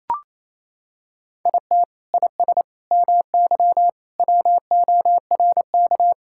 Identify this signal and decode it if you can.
Morse code, heard as a keyed tone, about 28 wpm, 700 Hz. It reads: IT IS MY WORK
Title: Sentences from Top 1000 Words - Review Part 5 - 28wpm